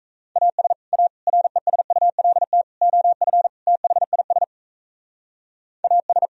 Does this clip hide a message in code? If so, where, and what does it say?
Yes: Morse code from 0.4 to 6.3 s, heard as a keyed tone, about 36 wpm, 700 Hz. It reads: AS A RESULT OF THIS AS